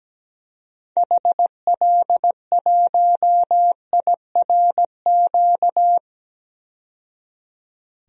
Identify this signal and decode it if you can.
Morse code, heard as a keyed tone, about 17 wpm, 700 Hz. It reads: HL1IRQ
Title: DXCC-100 Call Signs - 6 Characters - 17wpm